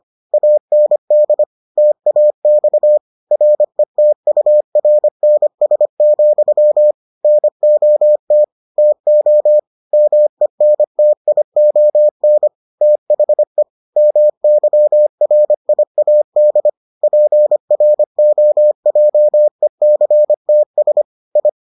Morse code, heard as a keyed tone, about 25 wpm, 600 Hz. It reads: AND TAX RETURNS, NOT TO MENTION THE MYRIAD PROJECTS I